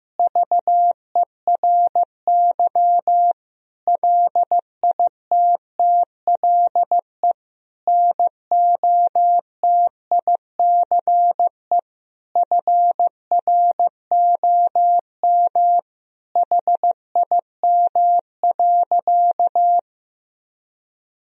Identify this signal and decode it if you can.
Morse code, heard as a keyed tone, about 15 wpm, 700 Hz. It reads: VERY LITTLE NOTICE FROM HIM.
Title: Sentences from Top 1000 Words - Review Part 2 - 15wpm